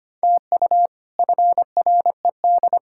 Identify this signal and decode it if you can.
Morse code, heard as a keyed tone, about 25 wpm, 700 Hz. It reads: TU FRED
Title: K1USN SST QSO Practice - 25wpm